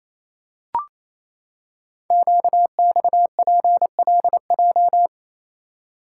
Morse code, heard as a keyed tone, about 29 wpm, 700 Hz. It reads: QXPLJ
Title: Code Groups - 5 Letters - 28wpm